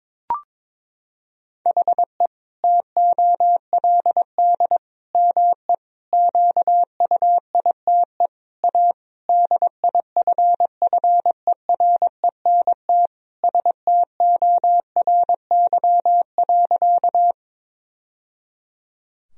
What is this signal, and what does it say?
Morse code, heard as a keyed tone, about 22 wpm, 700 Hz. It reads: HE TOLD ME QUITE A DIFFERENT STORY.
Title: Sentences from Top 1000 Words - Review Part 3 - 22wpm